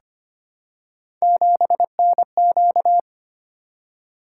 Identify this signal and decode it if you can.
Morse code, heard as a keyed tone, about 25 wpm, 700 Hz. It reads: 7NQ
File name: Three Letter-Number Combinations - 25wpm